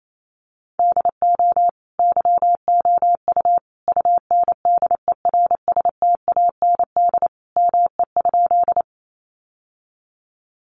Morse code, heard as a keyed tone, about 28 wpm, 700 Hz. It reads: DO YOU UNDERSTAND ME?
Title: Sentences from Top 600 Words 28wpm